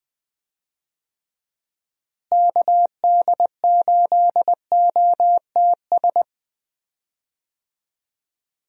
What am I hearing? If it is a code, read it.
Morse code, heard as a keyed tone, about 20 wpm, 700 Hz. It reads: KD8OTS
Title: USA Call Signs - 6 Characters - Repeated 3x - 20wpm